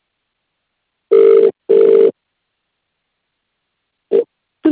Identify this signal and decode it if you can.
Ringback tone (UK), 400 + 450 Hz.